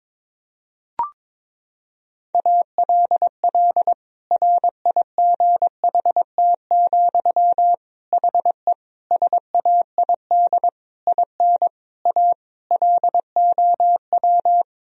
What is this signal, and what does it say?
Morse code, heard as a keyed tone, about 22 wpm, 700 Hz. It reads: ALL RIGHT, HE SAID IN A LOW
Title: Sentences from Top 1000 Words - Review Part 1 - 22wpm